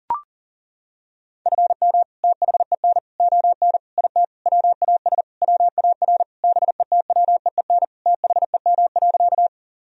Morse code, heard as a keyed tone, about 40 wpm, 700 Hz. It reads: FM THEN ON IT WAS WAR BETWEEN THEM.
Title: Sentences from Top 400 Words and Abbreviations 40wpm